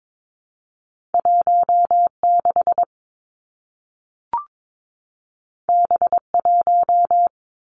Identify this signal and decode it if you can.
Morse code, heard as a keyed tone, about 22 wpm, 700 Hz. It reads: 1 6 B 1
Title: Two Challenging Letter-Number Combinations 22wpm